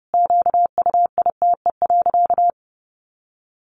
Morse code, heard as a keyed tone, about 30 wpm, 700 Hz. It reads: QUITE.
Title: Sentences from Top 1300 Words 30wpm